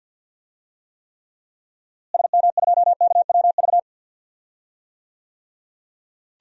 Morse code, heard as a keyed tone, about 48 wpm, 700 Hz. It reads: SM2KWV
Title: DXCC-100 Call Signs - 6 Characters - Repeated 2x - 50wpm